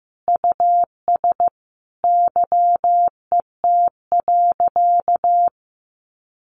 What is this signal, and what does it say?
Morse code, heard as a keyed tone, about 15 wpm, 700 Hz. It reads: US YET.
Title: A Sound Of Thunder 15wpm